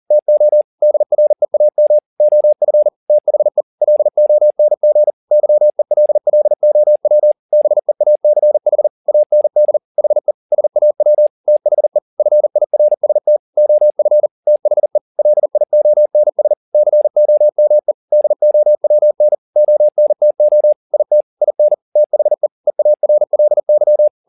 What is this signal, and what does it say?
Morse code, heard as a keyed tone, about 40 wpm, 600 Hz. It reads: TO DREAM OF THE LONG YELLOW BEACH AND HE SAW THE FIRST OF THE LIONS COME DOWN ONTO IT IN THE EARLY
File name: the old man and the sea 2 40wpm